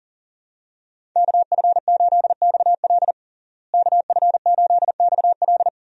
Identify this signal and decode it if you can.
Morse code, heard as a keyed tone, about 40 wpm, 700 Hz. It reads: KF8XL KF8XL
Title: USA Call Signs - 5 Characters - Repeated 2x - 40wpm